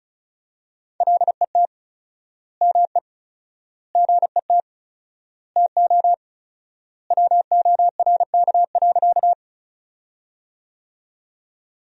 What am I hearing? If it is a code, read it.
Morse code, heard as a keyed tone, about 34 wpm, 700 Hz. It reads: LET ME GET TO WORK.
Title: 4x Word-Spacing - Sentences from Top 300 Words - 35wpm